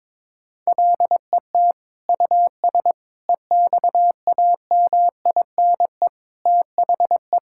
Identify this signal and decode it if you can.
Morse code, heard as a keyed tone, about 22 wpm, 700 Hz. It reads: LET US EXAMINE THE
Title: Sentences from Top 1600 Words 22wpm